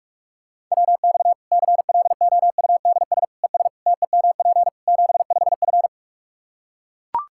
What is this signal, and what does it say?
Morse code, heard as a keyed tone, about 43 wpm, 700 Hz. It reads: WX CLOUDS ES TEMP 75F
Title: Practice QSO - Exchange 2 - Info - 45wpm